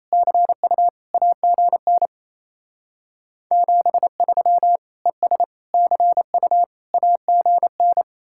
Morse code, heard as a keyed tone, about 34 wpm, 700 Hz. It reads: CU AGN 73 ES CU AGN
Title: Speed-Racing Common QSO Phrases - 28wpm